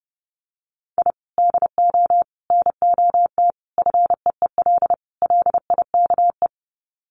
Morse code, heard as a keyed tone, about 30 wpm, 700 Hz. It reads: I DO NOT FEEL LIKE
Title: Sets of 5 Words Encore 30wpm